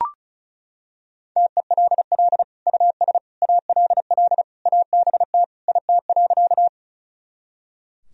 Morse code, heard as a keyed tone, about 36 wpm, 700 Hz. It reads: TELL US ALL ABT IT.
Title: Sentences from Top 200 Words and Abbreviations 35wpm